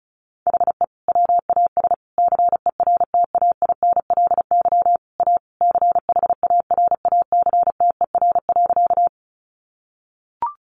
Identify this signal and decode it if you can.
Morse code, heard as a keyed tone, about 34 wpm, 700 Hz. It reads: HE WAS CERTAINLY A CHARACTER.
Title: Sentences from Top 1200 Words 35wpm